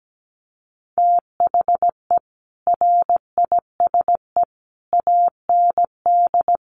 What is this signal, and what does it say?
Morse code, heard as a keyed tone, about 17 wpm, 700 Hz. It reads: THE RISE AND
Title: Sets of 5 Words Encore2 17wpm